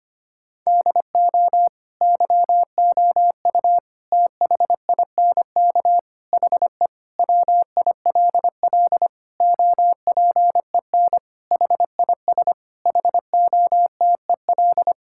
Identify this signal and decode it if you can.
Morse code, heard as a keyed tone, about 25 wpm, 700 Hz. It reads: DO YOU THINK HE WILL OPEN HIS HOTEL
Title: Sentences from Top 1600 Words 25wpm